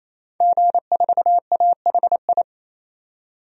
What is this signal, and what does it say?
Morse code, heard as a keyed tone, about 28 wpm, 700 Hz. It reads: G4AHI
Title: DXCC-100 Call Signs - 5 Characters - 28wpm